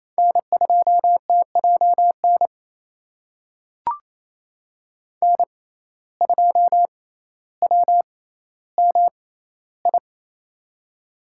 Morse code, heard as a keyed tone, about 28 wpm, spaced as for 14 wpm, 700 Hz. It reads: N2TJN N2WMI
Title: USA Call Signs - 5 Characters - 6x Character Spacing - 28wpm